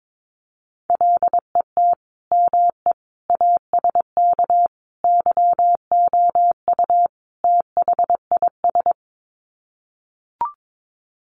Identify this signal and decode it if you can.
Morse code, heard as a keyed tone, about 22 wpm, 700 Hz. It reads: LET ME ASK YOU THIS
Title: Sets of 5 Words 22wpm